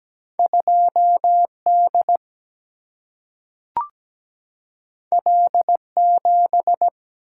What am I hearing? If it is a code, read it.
Morse code, heard as a keyed tone, about 17 wpm, 700 Hz. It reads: 2D L7
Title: Single-Letter Single-Number 17wpm